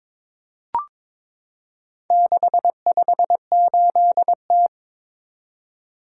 Morse code, heard as a keyed tone, about 23 wpm, 700 Hz. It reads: 658T
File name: practice / 22wpm / Four Letter-Number Combinations - 22wpm